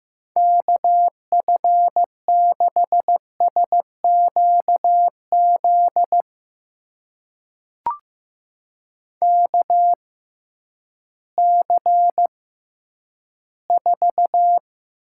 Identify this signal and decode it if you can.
Morse code, heard as a keyed tone, about 15 wpm, spaced as for 10 wpm, 700 Hz. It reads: KF6SQZ KC4
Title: USA Call Signs - 6 Characters - 6x Character Spacing - 15wpm